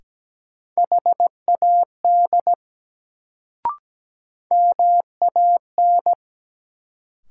Morse code, heard as a keyed tone, about 17 wpm, 700 Hz. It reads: HAD MAN
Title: Sets of 2 Words - New Word Drill - 17wpm